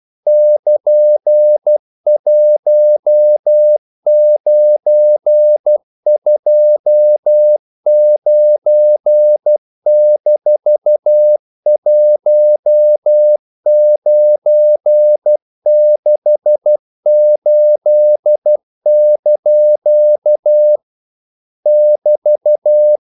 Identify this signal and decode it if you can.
Morse code, heard as a keyed tone, about 12 wpm, 600 Hz. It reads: <KN>1929-1968) <BT>
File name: CW / Practice / EnglishQuotations0112